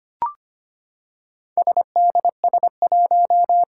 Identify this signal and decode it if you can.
Morse code, heard as a keyed tone, about 25 wpm, 700 Hz. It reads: SDS1
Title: Four Challenging Letter-Number Combinations 25wpm